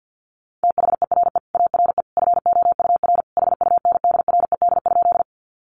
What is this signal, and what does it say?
Morse code, heard as a keyed tone, about 50 wpm, 700 Hz. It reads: THERE ARE FOUR HUNDRED?
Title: Sentences from Top 400 Words 50wpm